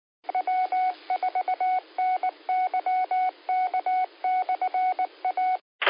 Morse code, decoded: W4NYK/A